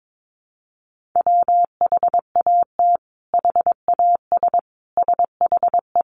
Morse code, heard as a keyed tone, about 22 wpm, 700 Hz. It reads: WHAT HAS SHE
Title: Sentences from Top 100 Words 22wpm